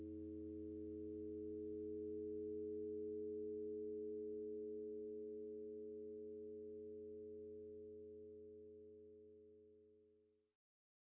Dial tone (Europe, 425 Hz), with noise.